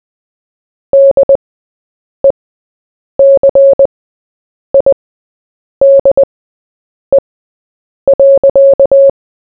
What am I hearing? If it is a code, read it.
Morse code, heard as a keyed tone, about 20 wpm, spaced as for 8 wpm, 550 Hz. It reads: DECIDE.
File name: Short-Story-4-8wpm